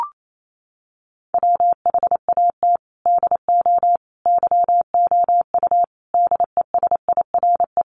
Morse code, heard as a keyed tone, about 28 wpm, 700 Hz. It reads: WHAT DO YOU DESIRE
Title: Sentences from Top 1100 Words 28wpm